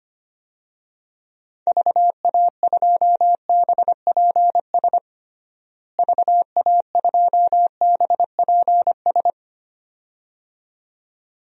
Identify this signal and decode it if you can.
Morse code, heard as a keyed tone, about 25 wpm, 700 Hz. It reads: VA2BPS VA2BPS
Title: DXCC-100 Call Signs - 6 Characters - Repeated 2x - 25wpm